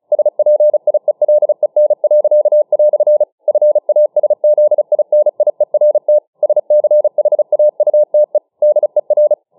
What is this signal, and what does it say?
Morse code, heard as a keyed tone, about 34 wpm, 600 Hz. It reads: SPIELEN." FASZINIERT SCHAUTE DER